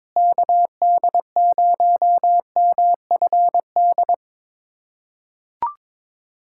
Morse code, heard as a keyed tone, about 22 wpm, 700 Hz. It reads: KD0MFD
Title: USA Callsign DE Callsign 22wpm